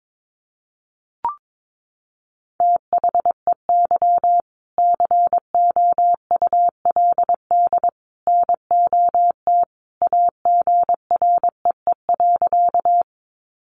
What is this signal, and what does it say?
Morse code, heard as a keyed tone, about 22 wpm, 700 Hz. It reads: THEY COULD NOT AGREE.
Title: Sentences from Top 1100 Words 22wpm